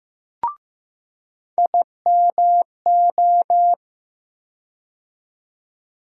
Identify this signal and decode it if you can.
Morse code, heard as a keyed tone, about 15 wpm, 700 Hz. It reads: IMO
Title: USA Call Sign Postfixes - 3 Letters 15wpm